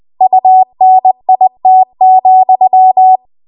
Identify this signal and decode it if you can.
Morse code, heard as a keyed tone, about 20 wpm, 750 Hz. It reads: UNIT,
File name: Morse / 20WPM